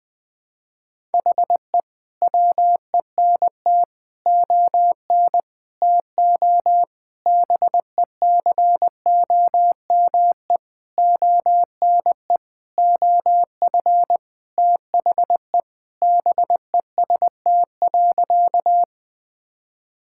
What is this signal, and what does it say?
Morse code, heard as a keyed tone, about 20 wpm, 700 Hz. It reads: HE WENT ON TO BECOME ONE OF THE BEST.
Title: Sentences with 7-Word Phrases - 20wpm